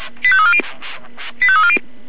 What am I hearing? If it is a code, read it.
Five-tone selective call (ZVEI), heard as quick runs of five tones, tones of about 70 ms. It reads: 85340